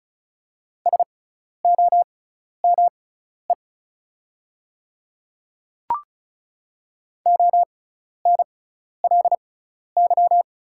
Morse code, heard as a keyed tone, about 36 wpm, spaced as for 12 wpm, 700 Hz. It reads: SOME ONLY
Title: 6x Character Spacing - Top 200 Words - 35wpm